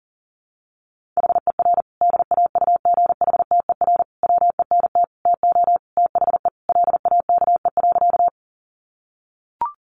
Morse code, heard as a keyed tone, about 37 wpm, 700 Hz. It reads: HER DAUGHTER WENT TO THE LAKE.
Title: Sentences from Top 1500 Words 40wpm